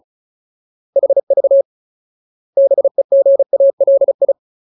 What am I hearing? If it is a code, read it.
Morse code, heard as a keyed tone, about 34 wpm, 550 Hz. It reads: HV BEGALI
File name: Ultimate Word List / 35WPM Ultimate